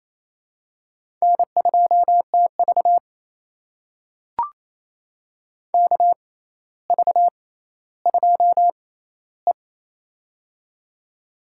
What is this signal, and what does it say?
Morse code, heard as a keyed tone, about 28 wpm, spaced as for 14 wpm, 700 Hz. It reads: N2TV KV2E
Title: USA Call Signs - 4 Characters - 6x Character Spacing - 28wpm